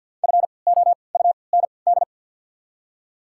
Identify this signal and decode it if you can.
Morse code, heard as a keyed tone, about 53 wpm, spaced as for 29 wpm, 700 Hz. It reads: FOUND
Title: 3x Character Spacing - Top 500 Words - 50wpm